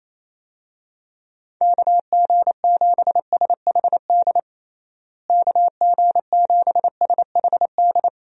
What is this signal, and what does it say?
Morse code, heard as a keyed tone, about 27 wpm, 700 Hz. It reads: KG7SHD KG7SHD
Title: USA Call Signs - 6 Characters - Repeated 2x - 28wpm